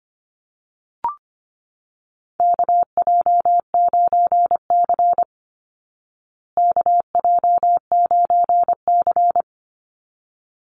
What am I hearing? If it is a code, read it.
Morse code, heard as a keyed tone, about 25 wpm, 700 Hz. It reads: KJ9C KJ9C
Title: Call Signs Repeated 3x - 25wpm